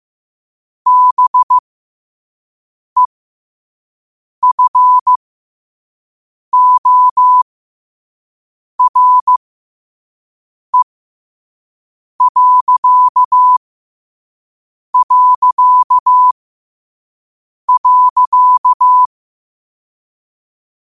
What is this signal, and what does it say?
Morse code, heard as a keyed tone, about 15 wpm, spaced as for 5 wpm, 1000 Hz. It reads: BEFORE...
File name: qotdfiles / 5wpm